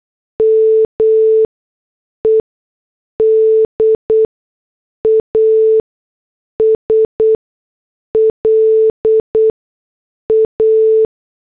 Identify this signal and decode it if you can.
Morse code, heard as a keyed tone, about 8 wpm, spaced as for 6 wpm, 440 Hz. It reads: MED AS LA